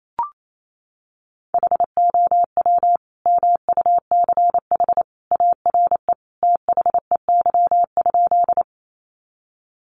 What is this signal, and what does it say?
Morse code, heard as a keyed tone, about 28 wpm, 700 Hz. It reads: HOW MUCH ARE THEY?
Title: Sentences from Top 200 Words 28wpm